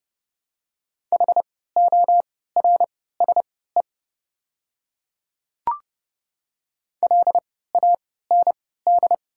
Morse code, heard as a keyed tone, about 30 wpm, spaced as for 17 wpm, 700 Hz. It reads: HORSE LAND
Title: 3x Character Spacing - Top 300 Words - 30wpm